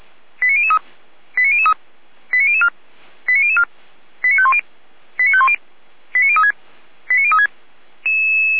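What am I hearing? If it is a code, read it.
Five-tone selective call (ZVEI), heard as quick runs of five tones, tones of about 70 ms. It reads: 89003 89003 89004 89004 89510 89510 89036 89036
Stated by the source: SDR recording from feuerwehr-orlamuende.org, fire brigade Orlamünde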